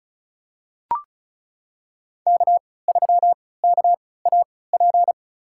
Morse code, heard as a keyed tone, about 36 wpm, spaced as for 20 wpm, 700 Hz. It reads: K3KAP
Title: USA Call Signs - 3x Character Spacing - 35wpm